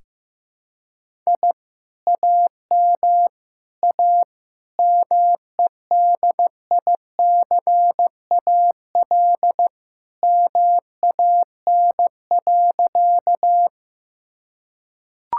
Morse code, heard as a keyed tone, about 16 wpm, 700 Hz. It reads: I AM A MEDICAL MAN.
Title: Sentences from Top 1000 Words - Review Part 5 - 15wpm